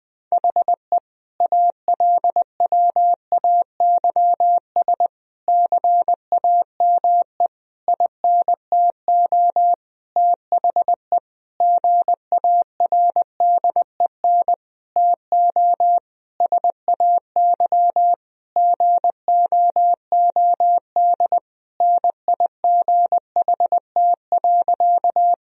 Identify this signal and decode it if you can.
Morse code, heard as a keyed tone, about 21 wpm, 700 Hz. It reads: HE ALWAYS CAME INTO THE GARDEN TO SAY GOOD NIGHT.
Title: Sentences from Top 1000 Words - Review Part 4 - 20wpm